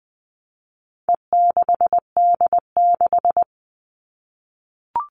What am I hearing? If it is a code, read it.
Morse code, heard as a keyed tone, about 20 wpm, 700 Hz. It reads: E6D6
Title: All Cut Numbers - 4-Digit 20wpm